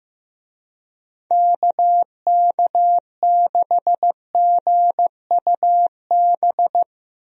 Morse code, heard as a keyed tone, about 15 wpm, 700 Hz. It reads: KK6GUB
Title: USA Call Signs - 6 Characters - Repeated 3x - 15wpm